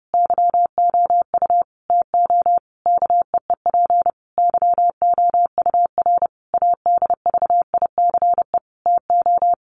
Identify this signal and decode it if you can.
Morse code, heard as a keyed tone, about 30 wpm, 700 Hz. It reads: YOU TO KEEP YOUR ADVICE TO
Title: Sentences from Top 1100 Words 30wpm